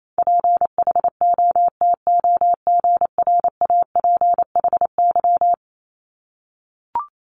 Morse code, heard as a keyed tone, about 28 wpm, 700 Hz. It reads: PHOTOGRAPHY
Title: Multi-letter Phonemes 28wpm